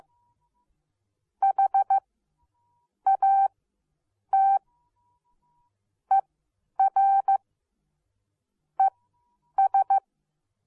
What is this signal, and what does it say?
Morse code, heard as a keyed tone, about 15 wpm, spaced as for 8 wpm, 780 Hz. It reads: HAT ER ES